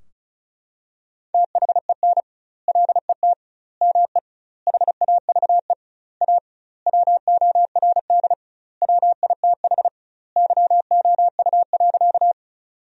Morse code, heard as a keyed tone, about 36 wpm, 700 Hz. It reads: THEN LET ME HAVE A WORD WITH YOU.
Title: 2x Word-Spacing - Sentences from Top 300 Words - 35wpm